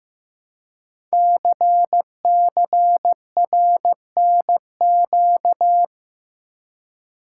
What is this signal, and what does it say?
Morse code, heard as a keyed tone, about 15 wpm, 700 Hz. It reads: CCRNQ
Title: Code Groups - 5 Letter:Numbers - 15wpm